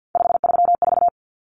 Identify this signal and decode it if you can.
Morse code, heard as a keyed tone, about 50 wpm, 700 Hz. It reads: 534
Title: Common Cut Numbers - 3-Digit 50wpm